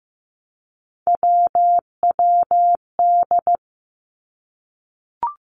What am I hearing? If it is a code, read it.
Morse code, heard as a keyed tone, about 15 wpm, 700 Hz. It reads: WWD
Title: Three Challenging Letter-Number Combinations 15wpm